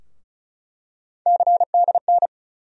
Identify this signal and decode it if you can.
Morse code, heard as a keyed tone, about 36 wpm, 700 Hz. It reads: CDN
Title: USA Call Sign Postfixes - 3 Letters 35wpm